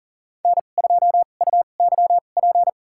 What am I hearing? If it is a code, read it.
Morse code, heard as a keyed tone, about 40 wpm, spaced as for 29 wpm, 700 Hz. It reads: N2UYP
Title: USA Call Signs - 5 Characters - 2x Character Spacing - 40wpm